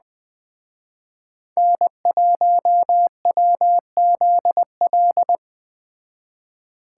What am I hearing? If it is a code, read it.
Morse code, heard as a keyed tone, about 19 wpm, 700 Hz. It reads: N1WZL